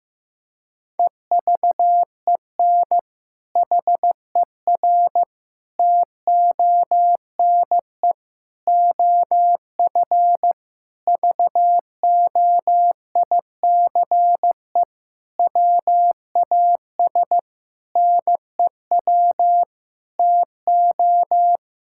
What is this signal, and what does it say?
Morse code, heard as a keyed tone, about 15 wpm, 700 Hz. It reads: EVEN HER TONE OF VOICE WAS NEW TO
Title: Sentences from Top 1600 Words 15wpm